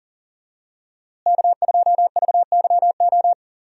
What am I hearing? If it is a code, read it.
Morse code, heard as a keyed tone, about 40 wpm, 700 Hz. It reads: K2VYO
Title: USA Call Signs - 5 Characters - Repeated 3x - 40wpm